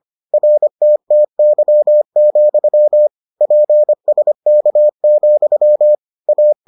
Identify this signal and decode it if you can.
Morse code, heard as a keyed tone, about 25 wpm, 600 Hz. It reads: RTTY, PSK, A